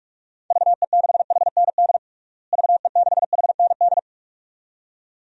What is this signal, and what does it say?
Morse code, heard as a keyed tone, about 46 wpm, 700 Hz. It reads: VE6HND VE6HND
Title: DXCC-100 Call Signs - 6 Characters - Repeated 2x - 45wpm